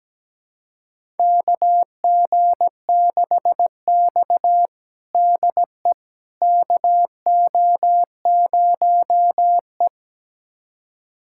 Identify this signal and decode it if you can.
Morse code, heard as a keyed tone, about 17 wpm, 700 Hz. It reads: KG6X DE KO0E
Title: USA Callsign DE Callsign - 4 Characters - 17wpm